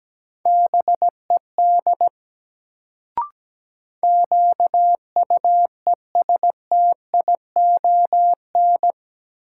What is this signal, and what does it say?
Morse code, heard as a keyed tone, about 17 wpm, 700 Hz. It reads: BED QUESTION
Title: Sets of 5 Words Encore - New Word Drill - 17wpm